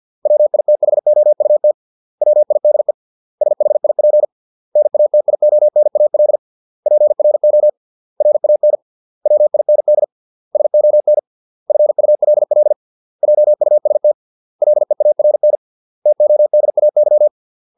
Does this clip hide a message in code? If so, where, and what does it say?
Yes: Morse code from 0.3 to 17.3 s, heard as a keyed tone, about 50 wpm, 600 Hz. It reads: WITHOUT WIDE SHIP NATIONAL PRO RAN WIND SON FULL JUST LEARN TODAY